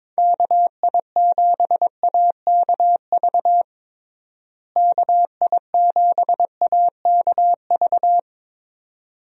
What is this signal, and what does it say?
Morse code, heard as a keyed tone, about 22 wpm, 700 Hz. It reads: KI7AKV KI7AKV
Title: USA Call Signs - 6 Characters - Repeated 2x - 22wpm